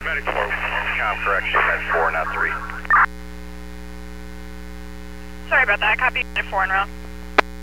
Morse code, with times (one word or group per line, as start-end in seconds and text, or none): none